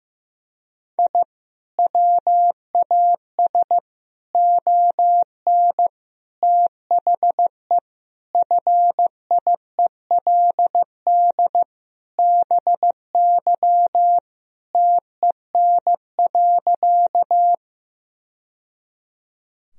Morse code, heard as a keyed tone, about 16 wpm, 700 Hz. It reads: I WAS ON THE FIELD BY TEN.
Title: Sentences from Top 1000 Words - Review Part 2 - 15wpm